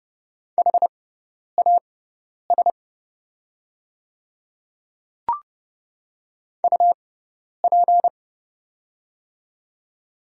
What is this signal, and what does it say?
Morse code, heard as a keyed tone, about 31 wpm, spaced as for 10 wpm, 700 Hz. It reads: HAS UP
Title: 6x Character Spacing - Top 300 Words - 30wpm